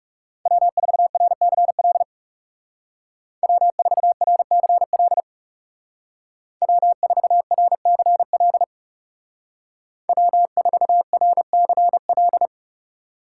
Morse code, heard as a keyed tone, about 37 wpm, 700 Hz. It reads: W4RCL W4RCL W4RCL W4RCL
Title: Speed Racing - USA 1x3 Call Signs 30wpm